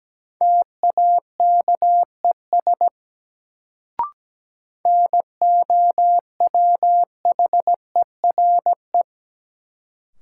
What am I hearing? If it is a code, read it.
Morse code, heard as a keyed tone, about 17 wpm, 700 Hz. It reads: TAKES NOWHERE
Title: Sets of 5 Words Encore - New Word Drill - 17wpm